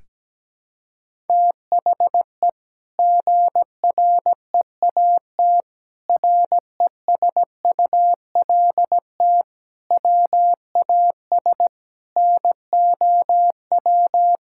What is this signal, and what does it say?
Morse code, heard as a keyed tone, about 17 wpm, 700 Hz. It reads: THE GREAT RESULT WAS NOW
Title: Sentences from Top 1000 Words - Review Part 5 - 17wpm